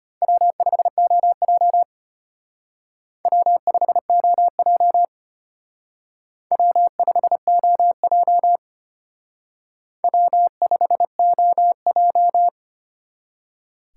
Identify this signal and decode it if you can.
Morse code, heard as a keyed tone, about 32 wpm, 700 Hz. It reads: W5OJ W5OJ W5OJ W5OJ